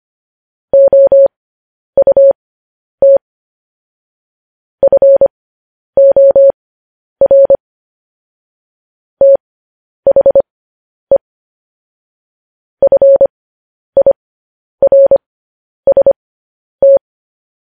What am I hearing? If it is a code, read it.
Morse code, heard as a keyed tone, about 25 wpm, spaced as for 10 wpm, 550 Hz. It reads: OUT FOR THE FIRST
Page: News Headlines in Morse Code at 10 WPM